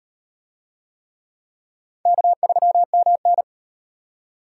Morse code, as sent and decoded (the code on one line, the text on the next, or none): -.- ...-- -- -.
K3MN